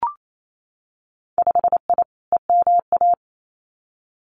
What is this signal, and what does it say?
Morse code, heard as a keyed tone, about 28 wpm, 700 Hz. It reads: 5I EMA
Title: Winter Field Day Exchanges Encore 28wpm